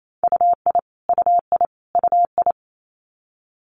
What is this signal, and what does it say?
Morse code, heard as a keyed tone, about 28 wpm, 700 Hz. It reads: UI UI UI
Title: Multi-letter Phonemes Repeated 3x 28wpm